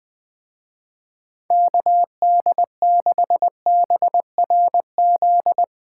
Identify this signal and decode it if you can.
Morse code, heard as a keyed tone, about 21 wpm, 700 Hz. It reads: KD6BRZ
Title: Speed Racing - USA 2x3 Call Signs 20wpm